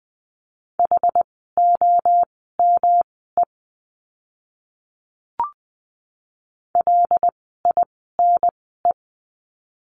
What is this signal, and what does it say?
Morse code, heard as a keyed tone, about 20 wpm, spaced as for 15 wpm, 700 Hz. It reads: HOME LINE
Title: 115 Lesson 19f - Letter w - Words 20wpm